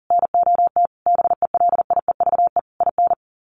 Morse code, heard as a keyed tone, about 40 wpm, 700 Hz. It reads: NOT BELIEVE IN
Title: Sets of 5 Words Encore 40wpm